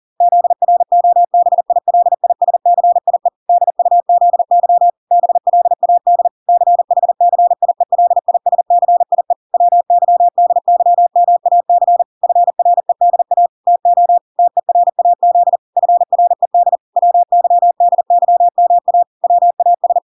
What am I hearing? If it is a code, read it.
Morse code, heard as a keyed tone, about 40 wpm, 700 Hz. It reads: ZROBILISCIE DUZY BLAD CHCIELISCIE WYDYMAC FREDA TO TERAZ FRED WYDYMA WAS